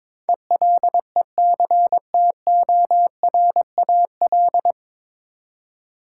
Morse code, heard as a keyed tone, about 21 wpm, 700 Hz. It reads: ELECTORAL
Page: Morse Code Ninja practice 22wpm